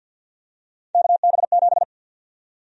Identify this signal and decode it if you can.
Morse code, heard as a keyed tone, about 52 wpm, 700 Hz. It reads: KB7
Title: USA Call Sign Prefixes - 50wpm